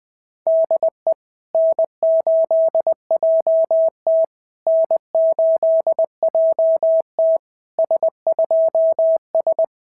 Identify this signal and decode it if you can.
Morse code, heard as a keyed tone, about 20 wpm, 690 Hz. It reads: DE N8JT N8JT S2S
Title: SOTA QSO Practice 20wpm